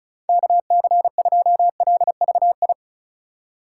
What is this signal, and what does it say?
Morse code, heard as a keyed tone, about 36 wpm, 690 Hz. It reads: KC2LVI
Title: USA Call Signs - 6 Characters - 2x Character Spacing - 35wpm